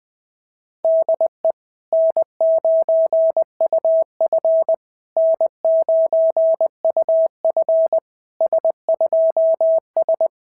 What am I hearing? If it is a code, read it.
Morse code, heard as a keyed tone, about 19 wpm, 650 Hz. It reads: DE N9UF N9UF S2S
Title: SOTA QSO Practice 20wpm